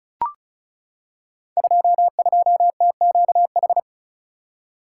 Morse code, heard as a keyed tone, about 36 wpm, 700 Hz. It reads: 22TQH
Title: Code Groups - 5 Letter:Numbers - 35wpm